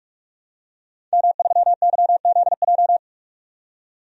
Morse code, heard as a keyed tone, about 41 wpm, 700 Hz. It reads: M3YZJ
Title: DXCC-100 Call Signs - 5 Characters - 45wpm